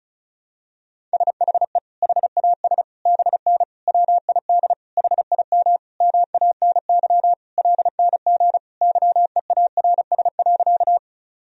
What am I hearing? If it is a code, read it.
Morse code, heard as a keyed tone, about 34 wpm, 700 Hz. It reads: SHE HAS BN WID HIM MANY LNG YEARS.
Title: Sentences from Top 200 Words and Abbreviations 35wpm